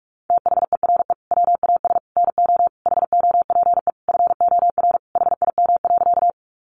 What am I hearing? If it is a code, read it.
Morse code, heard as a keyed tone, about 44 wpm, 700 Hz. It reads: THERE WAS NO HOPE FOR HIM.
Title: Sentences from Top 1000 Words 45wpm